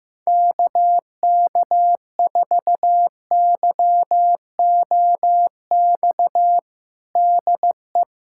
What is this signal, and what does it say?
Morse code, heard as a keyed tone, about 16 wpm, 700 Hz. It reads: KK4YOX DE
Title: USA Callsign DE Callsign 15wpm